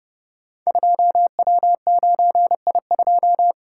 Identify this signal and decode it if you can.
Morse code, heard as a keyed tone, about 29 wpm, 700 Hz. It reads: 2W9I2
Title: Code Groups - 5 Letter:Numbers - 30wpm